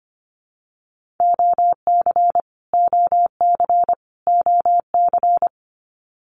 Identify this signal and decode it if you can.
Morse code, heard as a keyed tone, about 25 wpm, 700 Hz. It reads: OC OC OC
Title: Abbreviations Repeated 3x - 25wpm